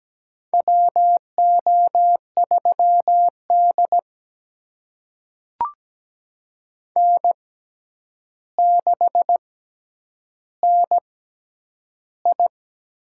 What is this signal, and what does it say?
Morse code, heard as a keyed tone, about 17 wpm, spaced as for 9 wpm, 700 Hz. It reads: WO3D N6NI
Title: USA Call Signs - 6x Character Spacing - 17wpm